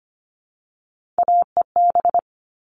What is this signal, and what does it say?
Morse code, heard as a keyed tone, about 25 wpm, 700 Hz. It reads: AEB
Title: All Cut Numbers - 3-Digit 25wpm